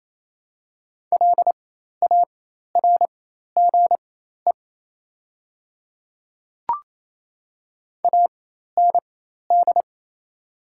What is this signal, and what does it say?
Morse code, heard as a keyed tone, about 28 wpm, spaced as for 13 wpm, 690 Hz. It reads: LARGE AND
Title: 4x Character Spacing - Top 200 Words - 28wpm